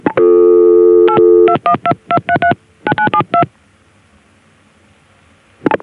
DTMF, heard as telephone touch tones, tones of about 80 ms. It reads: 82122339972